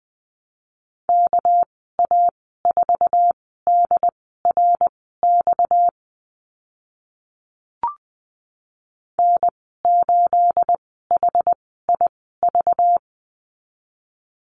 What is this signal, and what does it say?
Morse code, heard as a keyed tone, about 20 wpm, spaced as for 14 wpm, 700 Hz. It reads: KA4DRX N8HIV
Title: 238 Lesson 38g - Letter q - Call Signs 20wpm